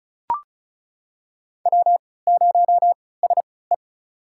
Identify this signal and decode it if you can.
Morse code, heard as a keyed tone, about 36 wpm, spaced as for 20 wpm, 700 Hz. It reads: W0SE
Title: USA Call Signs - 4 Characters - 3x Character Spacing - 35wpm